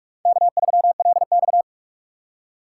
Morse code, heard as a keyed tone, about 47 wpm, 700 Hz. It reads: K3LX
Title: USA Call Signs - 4 Characters - 4x Character Spacing - 45wpm